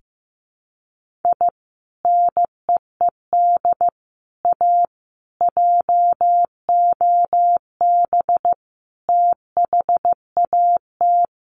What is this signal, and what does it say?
Morse code, heard as a keyed tone, about 15 wpm, 700 Hz. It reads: I NEED A JOB THAT
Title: Sentences from Top 1500 Words 15wpm